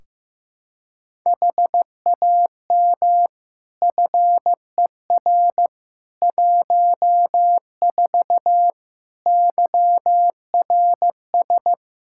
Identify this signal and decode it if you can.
Morse code, heard as a keyed tone, about 15 wpm, 700 Hz. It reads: HAM FER 14 YRS
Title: Practice QSO - Exchange 3 - More Info - 15wpm